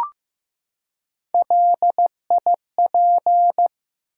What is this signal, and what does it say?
Morse code, heard as a keyed tone, about 15 wpm, 700 Hz. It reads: LIP
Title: Three Letter Words - 15wpm